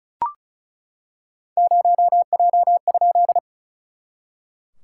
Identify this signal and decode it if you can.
Morse code, heard as a keyed tone, about 34 wpm, 690 Hz. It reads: 0J?
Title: SOTA QSO Practice 35wpm